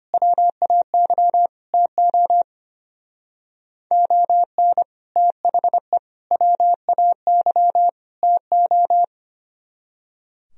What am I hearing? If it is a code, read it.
Morse code, heard as a keyed tone, about 30 wpm, 700 Hz. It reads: WAY TO ON THE WAY TO
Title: Speed-Racing - Sets of 4 Words 25wpm